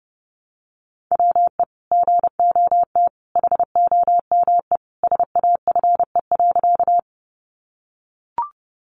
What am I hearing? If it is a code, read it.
Morse code, heard as a keyed tone, about 30 wpm, 700 Hz. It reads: WE GOT HOME SAFE.
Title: Sentences from Top 900 Words 30wpm